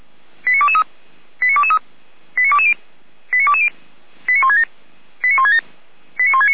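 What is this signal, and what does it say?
Five-tone selective call (ZVEI), heard as quick runs of five tones, tones of about 70 ms. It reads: 89223 89223 89220 89220 89167 89167 89169
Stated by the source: ZVEI five-tone call 89022 received on an SDR, from the fire brigade server of Feuerwehr Orlamünde.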